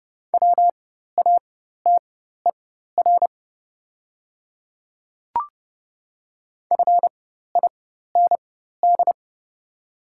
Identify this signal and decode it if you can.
Morse code, heard as a keyed tone, about 30 wpm, spaced as for 14 wpm, 700 Hz. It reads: WATER FIND
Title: 4x Character Spacing - Top 200 Words - 30wpm